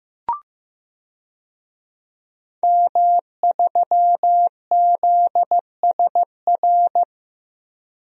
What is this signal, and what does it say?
Morse code, heard as a keyed tone, about 15 wpm, 700 Hz. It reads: M3ZSR
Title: DXCC-100 Call Signs - 5 Characters - 15wpm